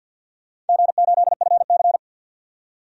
Morse code, heard as a keyed tone, about 52 wpm, 700 Hz. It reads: G8FC
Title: DXCC-100 Call Signs - 4 Characters - 50wpm